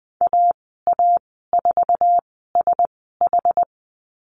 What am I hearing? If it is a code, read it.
Morse code, heard as a keyed tone, about 20 wpm, spaced as for 15 wpm, 700 Hz. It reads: AA4SH
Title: 154 Lesson 25g - Letter p - Call Signs 20wpm